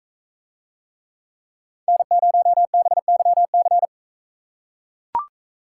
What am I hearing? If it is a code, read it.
Morse code, heard as a keyed tone, about 42 wpm, 700 Hz. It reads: N0BYC